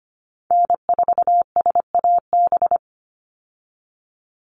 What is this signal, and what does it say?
Morse code, heard as a keyed tone, about 25 wpm, 700 Hz. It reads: N4SAB
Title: USA Call Signs 25wpm